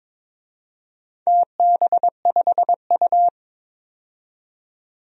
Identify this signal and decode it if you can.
Morse code, heard as a keyed tone, about 22 wpm, 700 Hz. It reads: TB5U
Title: DXCC-100 Call Signs - 4 Characters - 22wpm